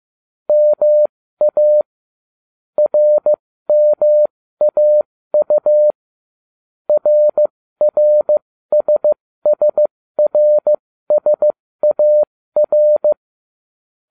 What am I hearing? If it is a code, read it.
Morse code, heard as a keyed tone, about 15 wpm, spaced as for 13 wpm, 600 Hz. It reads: MA RMAU RRSSRSAR